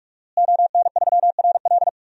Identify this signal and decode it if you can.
Morse code, heard as a keyed tone, about 46 wpm, 700 Hz. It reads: ON3RL
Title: DXCC-100 Call Signs - 5 Characters - Repeated 3x - 45wpm